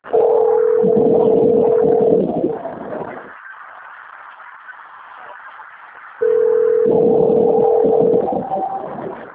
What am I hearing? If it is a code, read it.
Ringback tone (North America), 440 + 480 Hz, with noise.
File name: out- - - - - .209